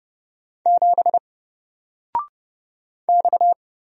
Morse code, heard as a keyed tone, about 29 wpm, 700 Hz. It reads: 7 X
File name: Single-Letter-Number-30wpm